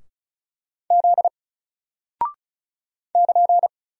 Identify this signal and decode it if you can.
Morse code, heard as a keyed tone, about 34 wpm, 700 Hz. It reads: Z <KN>
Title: Single Character (Letter:Number:Punctuation:Prosign) Alt 35wpm